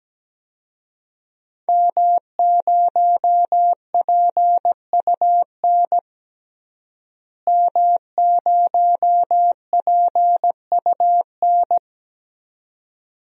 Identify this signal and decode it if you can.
Morse code, heard as a keyed tone, about 17 wpm, 700 Hz. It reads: M0PUN M0PUN
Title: DXCC-100 Call Signs - 5 Characters - Repeated 2x - 17wpm